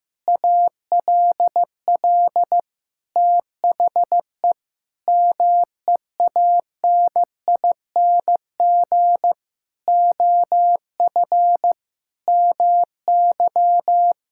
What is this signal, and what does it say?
Morse code, heard as a keyed tone, about 16 wpm, 700 Hz. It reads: ALL THE MEANING OF MY
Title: Sentences from Top 1000 Words - Review Part 6 - 15wpm